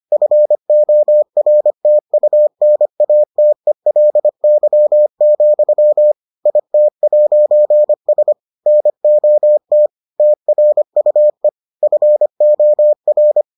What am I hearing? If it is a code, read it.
Morse code, heard as a keyed tone, about 25 wpm, 600 Hz. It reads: FORTUNATELY, IT'S NOT TRUE FOR